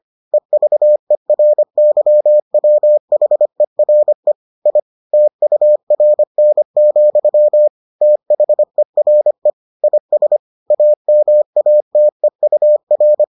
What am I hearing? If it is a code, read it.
Morse code, heard as a keyed tone, about 25 wpm, 600 Hz. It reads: EVERYWHERE I TURN, THERE IS AMATEUR